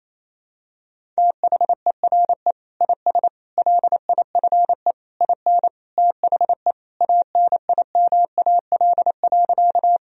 Morse code, heard as a keyed tone, about 28 wpm, 700 Hz. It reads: THERE IS LIFE IN THE ANIMAL.
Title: Sentences from Top 1000 Words - Review Part 1 - 28wpm